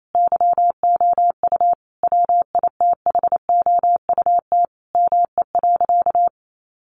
Morse code, heard as a keyed tone, about 28 wpm, 700 Hz. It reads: YOU WITHOUT ME.
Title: Sentences from Top 1200 Words 28wpm